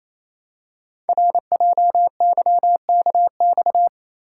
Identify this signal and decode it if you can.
Morse code, heard as a keyed tone, about 29 wpm, 700 Hz. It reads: RJYKX
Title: Code Groups - 5 Letters - 28wpm